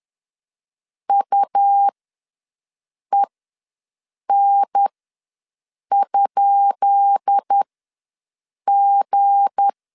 Morse code, heard as a keyed tone, about 10 wpm, spaced as for 6 wpm, 800 Hz. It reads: UEN?G
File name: LEZIONI / 10 SIGNS / 10D COMA ws 10wpm 4314